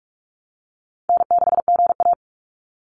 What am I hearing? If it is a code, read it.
Morse code, heard as a keyed tone, about 44 wpm, 700 Hz. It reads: N6GA
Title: Easy Call Signs Repeated 2x - 45wpm